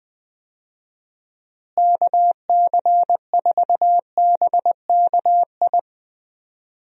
Morse code, heard as a keyed tone, about 20 wpm, 700 Hz. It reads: KC4BKI
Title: USA Call Signs - 6 Characters - Repeated 3x - 20wpm